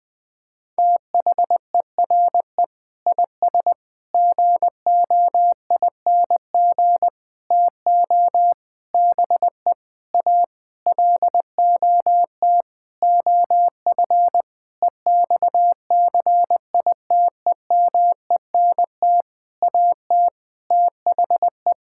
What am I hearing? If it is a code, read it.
Morse code, heard as a keyed tone, about 20 wpm, 700 Hz. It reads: THERE IS GOING TO BE A LOT OF EXCITEMENT AT THE
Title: Sentences with 7-Word Phrases - 20wpm